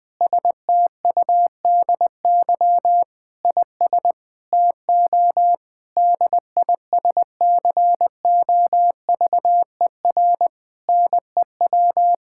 Morse code, heard as a keyed tone, about 20 wpm, 700 Hz. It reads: STUDY IS TO DISCOVER NEW
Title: Sentences with 5-Word Phrases - 20wpm